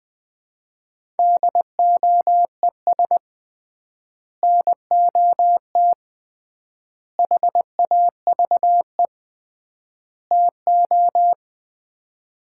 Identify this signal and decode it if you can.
Morse code, heard as a keyed tone, about 21 wpm, 700 Hz. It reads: DOES NOT HAVE TO
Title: 3x Word-Spacing - Sets of 4 Words - 20wpm